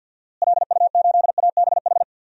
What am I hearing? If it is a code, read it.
Morse code, heard as a keyed tone, about 48 wpm, 700 Hz. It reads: LU8ABH
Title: DXCC-100 Call Signs - 6 Characters - Repeated 2x - 50wpm